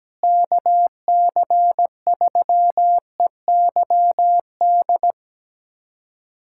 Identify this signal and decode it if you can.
Morse code, heard as a keyed tone, about 17 wpm, 700 Hz. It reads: KC3EYD
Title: USA Callsign DE Callsign 17wpm